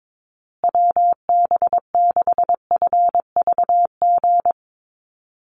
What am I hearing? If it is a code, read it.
Morse code, heard as a keyed tone, about 22 wpm, 700 Hz. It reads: WB6FVG
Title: USA Call Signs Repeated 3x 22wpm